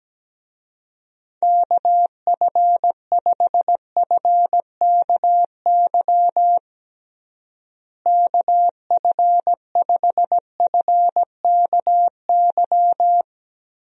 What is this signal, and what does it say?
Morse code, heard as a keyed tone, about 17 wpm, 700 Hz. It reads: KF5FKY KF5FKY
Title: USA Call Signs - 6 Characters - Repeated 2x - 17wpm